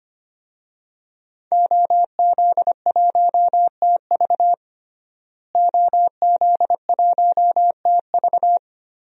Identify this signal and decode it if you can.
Morse code, heard as a keyed tone, about 25 wpm, 700 Hz. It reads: OZ1TV OZ1TV
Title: DXCC-100 Call Signs - 5 Characters - Repeated 3x - 25wpm